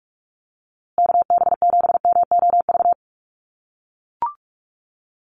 Morse code, heard as a keyed tone, about 44 wpm, 700 Hz. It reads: KB7MOV
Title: USA Call Signs Repeated 3x 45wpm